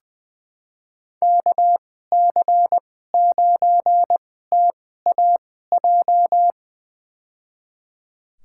Morse code, heard as a keyed tone, about 20 wpm, spaced as for 14 wpm, 700 Hz. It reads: KC9TAJ
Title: USA Call Signs - 6 Characters - 2x Character Spacing - 20wpm